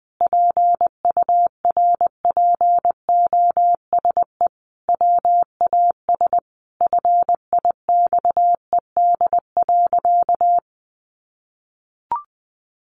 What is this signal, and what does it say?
Morse code, heard as a keyed tone, about 20 wpm, 700 Hz. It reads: PURPOSE WAS FIXED.
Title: Sentences from Top 1100 Words 20wpm